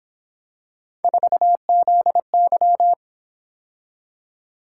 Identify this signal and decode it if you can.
Morse code, heard as a keyed tone, about 26 wpm, 700 Hz. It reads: 4ZY
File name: Speed-Racing - Three Letter-Number Combinations - 22wpm